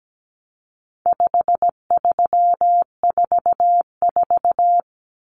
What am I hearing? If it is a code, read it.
Morse code, heard as a keyed tone, about 17 wpm, 700 Hz. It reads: 5344
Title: Numbers - 4-Digit 17wpm